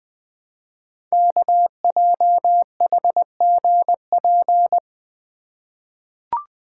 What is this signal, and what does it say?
Morse code, heard as a keyed tone, about 20 wpm, 700 Hz. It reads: KJHGP
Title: Code Groups - 5 Letters - 20wpm